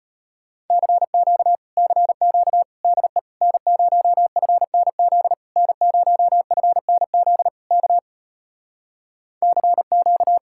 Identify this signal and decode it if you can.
Morse code, heard as a keyed tone, about 33 wpm, 700 Hz. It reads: CQ CQ DE N0FNZ N0FNZ K CQ